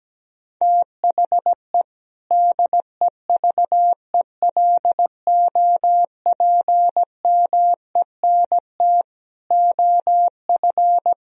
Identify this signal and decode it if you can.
Morse code, heard as a keyed tone, about 17 wpm, 700 Hz. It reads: THE DEVELOPMENT OF
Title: Sentences from Top 1000 Words - Review Part 4 - 17wpm